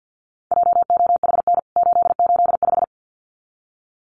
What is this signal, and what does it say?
Morse code, heard as a keyed tone, about 49 wpm, 700 Hz. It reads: JOHN 885
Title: CWT Exchange 50wpm